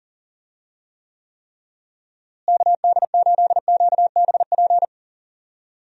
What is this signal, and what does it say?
Morse code, heard as a keyed tone, about 37 wpm, 700 Hz. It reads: KD8QBP
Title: USA Call Signs - 6 Characters - Repeated 3x - 40wpm